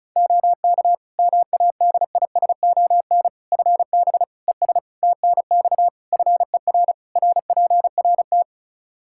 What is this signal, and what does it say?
Morse code, heard as a keyed tone, about 34 wpm, 700 Hz. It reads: OK MADISON FB ES TNX FER RPRT
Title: Practice QSO - Exchange 2 - Info - 35wpm